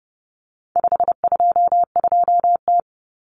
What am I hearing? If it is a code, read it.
Morse code, heard as a keyed tone, about 30 wpm, 700 Hz. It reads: 522T
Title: Common Cut Numbers - 4-Digit 30wpm